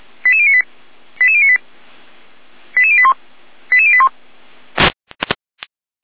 Five-tone selective call (ZVEI), heard as quick runs of five tones, tones of about 70 ms. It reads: 88098 88098 88081 88081
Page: /455/SDR/ZVEI/